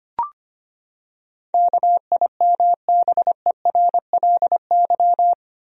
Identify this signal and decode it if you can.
Morse code, heard as a keyed tone, about 25 wpm, 700 Hz. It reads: KIMBERLY
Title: USA Common Names - 25wpm